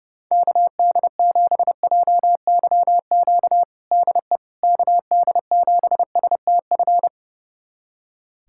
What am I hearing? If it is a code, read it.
Morse code, heard as a keyed tone, about 32 wpm, 700 Hz. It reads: KD7JYQ DE KD7STF
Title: USA Callsign DE Callsign 30wpm